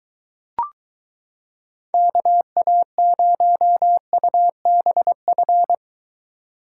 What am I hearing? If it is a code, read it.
Morse code, heard as a keyed tone, about 23 wpm, 700 Hz. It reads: KA0UBF